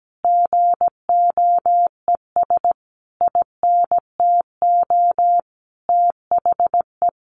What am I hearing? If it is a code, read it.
Morse code, heard as a keyed tone, about 17 wpm, 700 Hz. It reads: GOES INTO THE
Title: Easy Phrases 17wpm